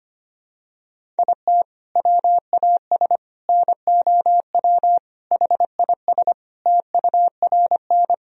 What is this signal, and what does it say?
Morse code, heard as a keyed tone, about 25 wpm, 700 Hz. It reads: IT WAS NOW HIS TURN
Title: Sentences from Top 200 Words and Abbreviations 25wpm